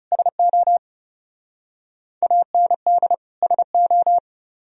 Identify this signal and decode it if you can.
Morse code, heard as a keyed tone, about 38 wpm, 700 Hz. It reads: SO AND SO
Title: Speed-Racing - Sets of 2 Words 30wpm